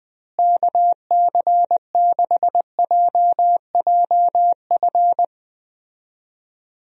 Morse code, heard as a keyed tone, about 20 wpm, 700 Hz. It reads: KC6JJF
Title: USA Call Signs - 6 Characters - Repeated 2x - 20wpm